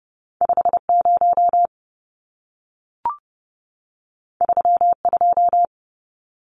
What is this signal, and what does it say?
Morse code, heard as a keyed tone, about 30 wpm, 700 Hz. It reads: 50 32
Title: Numbers - 2-Digit 30wpm